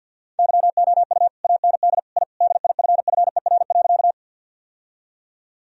Morse code, heard as a keyed tone, about 52 wpm, 700 Hz. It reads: YOU AND I DIFFER.
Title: Sentences from Top 1000 Words - Review Part 1 - 50wpm